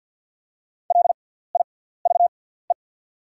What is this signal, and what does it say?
Morse code, heard as a keyed tone, about 48 wpm, spaced as for 17 wpm, 700 Hz. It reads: LIVE
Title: 6x Character Spacing - Top 200 Words - 50wpm